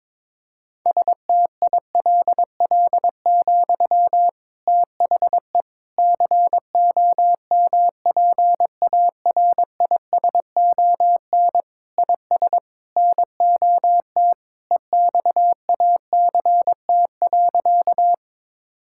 Morse code, heard as a keyed tone, about 22 wpm, 700 Hz. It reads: STILL, THE COMPARISON IS NOT EXACT.
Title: Sentences from Top 1600 Words 22wpm